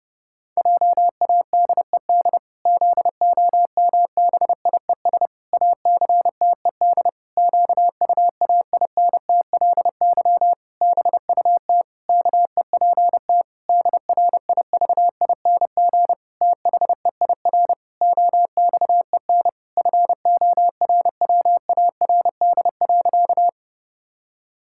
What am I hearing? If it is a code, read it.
Morse code, heard as a keyed tone, about 30 wpm, 700 Hz. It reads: JADED ZOMBIES ACTED QUAINTLY BUT KEPT DRIVING THEIR OXEN FORWARD.
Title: Pangram Challenge v2 - 30wpm